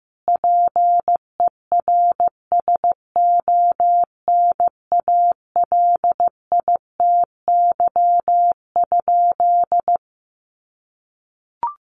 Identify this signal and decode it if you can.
Morse code, heard as a keyed tone, about 15 wpm, 700 Hz. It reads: PERSONALITY?
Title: Sentences from Top 1400 Words 15wpm